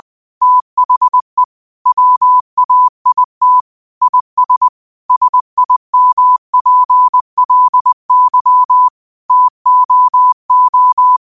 Morse code, heard as a keyed tone, about 20 wpm, 1000 Hz. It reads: THE WAIT IS SIMPLY TOO